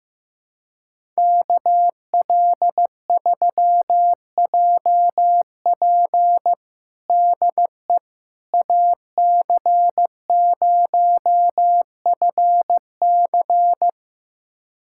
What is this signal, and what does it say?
Morse code, heard as a keyed tone, about 15 wpm, 700 Hz. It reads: KL3JP DE AC0FC
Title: USA Callsign DE Callsign - 5 Characters - 15wpm